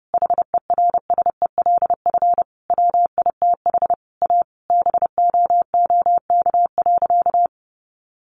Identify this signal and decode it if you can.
Morse code, heard as a keyed tone, about 30 wpm, 700 Hz. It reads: HERSELF WITH A BOOK.
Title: Easy Phrases 30wpm